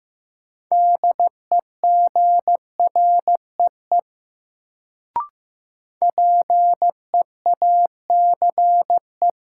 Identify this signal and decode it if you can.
Morse code, heard as a keyed tone, about 15 wpm, 700 Hz. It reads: DEGREE PEACE
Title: Sentences from Top 1500 Words - New Word Drill - 15wpm